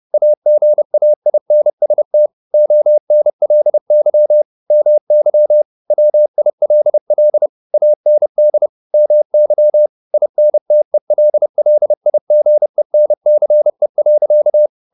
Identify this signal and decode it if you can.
Morse code, heard as a keyed tone, about 30 wpm, 600 Hz. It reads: AGAINST ONLY MY WILL AND MY INTELLIGENCE.
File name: the old man and the sea / the old man and the sea 1 30wpm